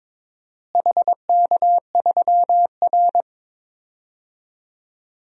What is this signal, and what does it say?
Morse code, heard as a keyed tone, about 21 wpm, 700 Hz. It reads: HK3R
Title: DXCC-100 Call Signs - 4 Characters - Repeated 3x - 22wpm